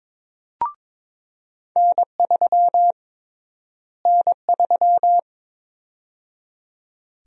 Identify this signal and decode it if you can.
Morse code, heard as a keyed tone, about 23 wpm, 700 Hz. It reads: N3 N3
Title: USA Call Sign Prefixes - Repeated 2x - 22wpm